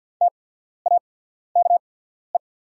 Morse code, heard as a keyed tone, about 48 wpm, spaced as for 14 wpm, 700 Hz. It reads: TAKE
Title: 8x Character Spacing - Top 500 Words - 50wpm